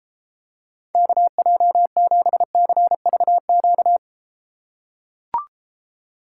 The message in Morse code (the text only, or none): KJ7CVQ